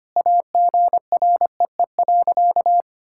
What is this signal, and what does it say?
Morse code, heard as a keyed tone, about 25 wpm, 700 Hz. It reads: AGREE.
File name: practice / 25wpm / Sentences with 3-Word Phrases - 25wpm